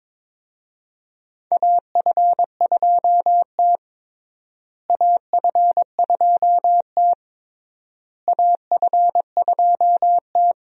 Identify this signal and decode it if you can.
Morse code, heard as a keyed tone, about 22 wpm, 700 Hz. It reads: AF2T AF2T AF2T
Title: USA Call Signs - 4 Characters - Repeated 3x - 22wpm